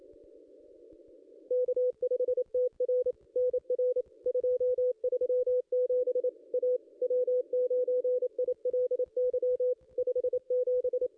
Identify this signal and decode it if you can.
Morse code, heard as a keyed tone, about 26 wpm, 500 Hz. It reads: K5TR NR 237 A W9ILY 57
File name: contests / ss-cw-k5tr-op-n5rz -2200